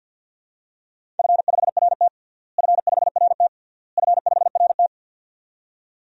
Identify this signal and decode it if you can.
Morse code, heard as a keyed tone, about 52 wpm, 700 Hz. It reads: F5RT F5RT F5RT
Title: DXCC-100 Call Signs - 4 Characters - Repeated 3x - 50wpm